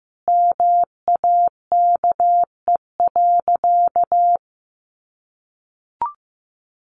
Morse code, heard as a keyed tone, about 15 wpm, 700 Hz. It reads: MAKE.
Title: Sentences from Top 1000 Words 15wpm